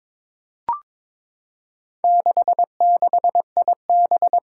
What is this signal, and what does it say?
Morse code, heard as a keyed tone, about 23 wpm, 700 Hz. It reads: 66IB
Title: Four Challenging Letter-Number Combinations 22wpm